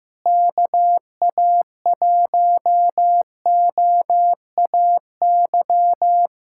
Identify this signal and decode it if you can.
Morse code, heard as a keyed tone, about 15 wpm, 700 Hz. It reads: KA1OAY
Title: USA Callsign DE Callsign 15wpm